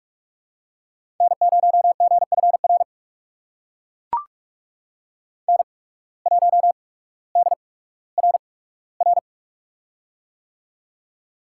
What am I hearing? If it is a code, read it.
Morse code, heard as a keyed tone, about 44 wpm, spaced as for 19 wpm, 700 Hz. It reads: N0GFR N1DRR
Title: USA Call Signs - 5 Characters - 8x Character Spacing - 45wpm